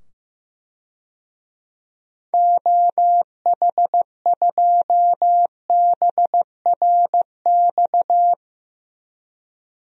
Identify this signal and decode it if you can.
Morse code, heard as a keyed tone, about 16 wpm, 700 Hz. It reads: OH2BRX
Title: DXCC-100 Call Signs - 6 Characters - 15wpm